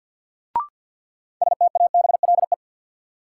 Morse code, heard as a keyed tone, about 53 wpm, 700 Hz. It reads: STABLE
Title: Sentences from Top 1500 Words - New Word Drill - 50wpm